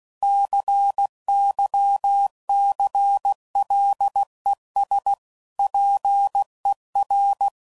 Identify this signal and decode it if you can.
Morse code, heard as a keyed tone, about 16 wpm, 800 Hz. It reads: CYCLES PER
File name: CW Code Groups @15WPM / Technical Subjects / Reactance